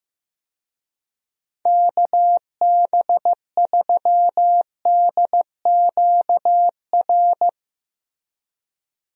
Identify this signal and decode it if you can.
Morse code, heard as a keyed tone, about 15 wpm, 700 Hz. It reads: KB3DQR
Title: USA Call Signs - 6 Characters - Repeated 3x - 15wpm